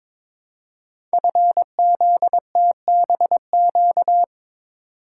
Morse code, heard as a keyed tone, about 23 wpm, 700 Hz. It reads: FZTBQ
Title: Code Groups - 5 Letters - 22wpm